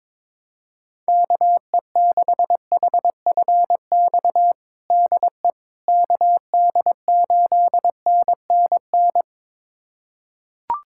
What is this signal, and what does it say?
Morse code, heard as a keyed tone, about 22 wpm, 700 Hz. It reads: KE6HFX DE KD8NNN
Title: USA Callsign DE Callsign - 6 Characters - 22wpm